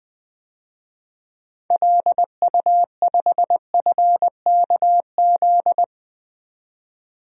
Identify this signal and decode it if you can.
Morse code, heard as a keyed tone, about 20 wpm, 700 Hz. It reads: LU5FKZ
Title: DXCC-100 Call Signs - 6 Characters - 20wpm